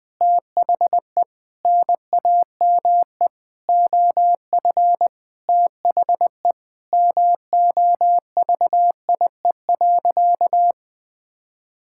Morse code, heard as a keyed tone, about 21 wpm, 700 Hz. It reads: THE NAME OF THE MOVIE.
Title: Sentences with 4-Word Phrases - 20wpm